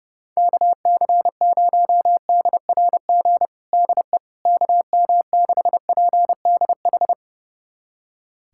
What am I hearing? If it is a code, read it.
Morse code, heard as a keyed tone, about 31 wpm, 700 Hz. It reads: KC0DRG DE KM6PDH
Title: USA Callsign DE Callsign - 6 Characters - 30wpm